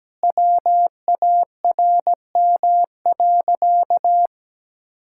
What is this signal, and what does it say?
Morse code, heard as a keyed tone, about 17 wpm, 700 Hz. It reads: WARM.
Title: Sentences from Top 1600 Words 17wpm